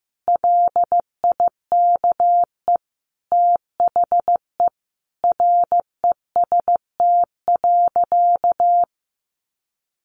Morse code, heard as a keyed tone, about 14 wpm, 700 Hz. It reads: LIKE THE REST.
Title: Sentences from Top 1100 Words 15wpm